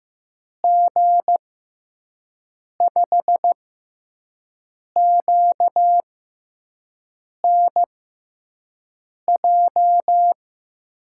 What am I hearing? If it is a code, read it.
Morse code, heard as a keyed tone, about 15 wpm, spaced as for 5 wpm, 700 Hz. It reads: G5QNJ
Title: USA Call Signs - 6 Characters - 6x Character Spacing - 15wpm